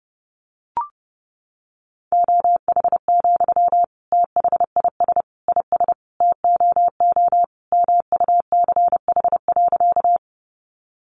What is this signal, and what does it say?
Morse code, heard as a keyed tone, about 30 wpm, 700 Hz. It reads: OH, THIS IS TOO MUCH.
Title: Sentences from Top 500 Words 30wpm